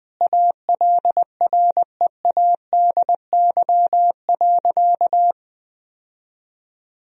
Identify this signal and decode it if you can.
Morse code, heard as a keyed tone, about 20 wpm, 700 Hz. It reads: ALREADY.
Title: Sentences from Top 1000 Words - Review Part 3 - 20wpm